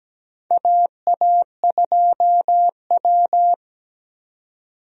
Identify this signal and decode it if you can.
Morse code, heard as a keyed tone, about 17 wpm, 700 Hz. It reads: AA2W
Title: USA Call Signs - 4 Characters - 17wpm